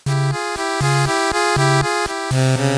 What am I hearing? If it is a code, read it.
There is no code.